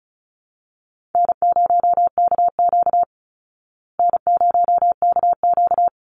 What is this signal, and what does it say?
Morse code, heard as a keyed tone, about 34 wpm, 700 Hz. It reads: N0KQ N0KQ
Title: Easy Call Signs Repeated 2x - 35wpm